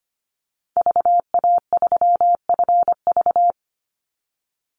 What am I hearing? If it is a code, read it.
Morse code, heard as a keyed tone, about 25 wpm, 700 Hz. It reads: VA3FV
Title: Call Signs - 25wpm